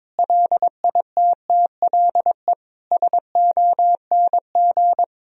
Morse code, heard as a keyed tone, about 22 wpm, 700 Hz. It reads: LITTLE SONG
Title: Sentences from Top 1000 Words - Review Part 2 - 22wpm